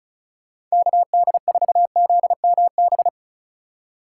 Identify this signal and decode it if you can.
Morse code, heard as a keyed tone, about 34 wpm, 700 Hz. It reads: KD4ZMB
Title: USA Call Signs - 6 Characters - 3x Character Spacing - 35wpm